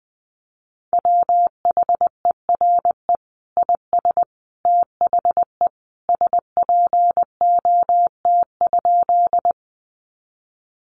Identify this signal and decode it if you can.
Morse code, heard as a keyed tone, about 20 wpm, 700 Hz. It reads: WHERE IS THE SPOT?
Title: Sentences from Top 1100 Words 20wpm